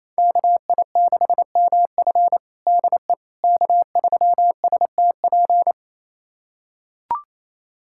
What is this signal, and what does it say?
Morse code, heard as a keyed tone, about 28 wpm, 700 Hz. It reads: KI6MF DE K3STP
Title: USA Callsign DE Callsign - 5 Characters - 28wpm